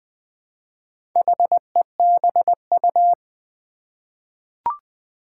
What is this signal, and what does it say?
Morse code, heard as a keyed tone, about 20 wpm, 700 Hz. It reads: HEBU
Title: Four Challenging Letter-Number Combinations 20wpm